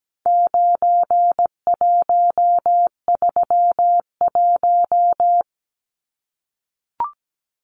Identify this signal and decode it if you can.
Morse code, heard as a keyed tone, about 17 wpm, 700 Hz. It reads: 9131
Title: Numbers - 4-Digit 17wpm